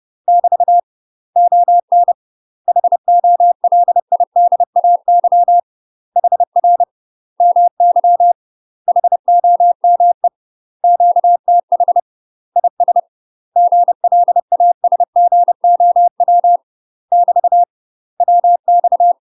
Morse code, heard as a keyed tone, about 30 wpm, 700 Hz. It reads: <BT> ON HOLIDAY HR MY HOME QTH IS GLASGOW <BT> WX
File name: UK MASTER QSOs TRAINING/30WPM